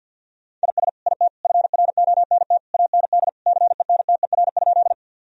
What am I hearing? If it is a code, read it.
Morse code, heard as a keyed tone, about 53 wpm, 700 Hz. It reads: IS IT FRONT AND CENTER?
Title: Sentences from Top 1000 Words - Review Part 2 - 50wpm